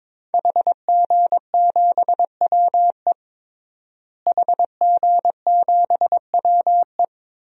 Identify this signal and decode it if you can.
Morse code, heard as a keyed tone, about 22 wpm, 700 Hz. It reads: HG7WE HG7WE
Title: DXCC-100 Call Signs - 5 Characters - Repeated 2x - 22wpm